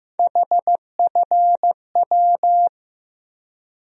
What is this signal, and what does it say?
Morse code, heard as a keyed tone, about 15 wpm, 700 Hz. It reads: HFW
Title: USA Call Sign Postfixes - 3 Letters 15wpm